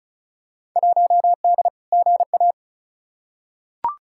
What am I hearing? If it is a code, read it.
Morse code, heard as a keyed tone, about 36 wpm, 700 Hz. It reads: 1D GA
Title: Field Day Exchanges - 35wpm